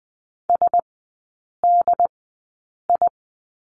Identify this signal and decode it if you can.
Morse code, heard as a keyed tone, about 20 wpm, spaced as for 8 wpm, 700 Hz. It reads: SDI
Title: 067 Lesson 12e - Letter d - Three Characters 20wpm